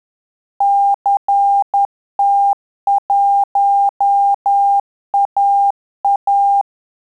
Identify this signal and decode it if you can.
Morse code, heard as a keyed tone, about 10 wpm, 800 Hz. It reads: CT1AA
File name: Telegraphy-group / QSO CW 10 PPM